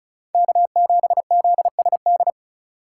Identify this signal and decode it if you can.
Morse code, heard as a keyed tone, about 36 wpm, 700 Hz. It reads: K7ZSD
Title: USA Call Signs - 5 Characters - 2x Character Spacing - 35wpm